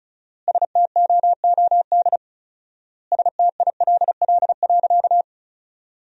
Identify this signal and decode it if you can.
Morse code, heard as a keyed tone, about 36 wpm, 700 Hz. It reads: STOOD STILL.
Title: 4x Word-Spacing - Sentences from Top 500 Words - 35wpm